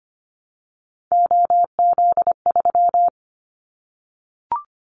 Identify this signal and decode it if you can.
Morse code, heard as a keyed tone, about 25 wpm, 700 Hz. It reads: OZ3
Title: Callsign Prefixes for Top 100 DXCC - 25wpm